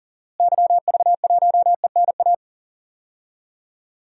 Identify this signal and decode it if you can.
Morse code, heard as a keyed tone, about 37 wpm, 700 Hz. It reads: YV1ENA
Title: DXCC-100 Call Signs - 6 Characters - Repeated 2x - 40wpm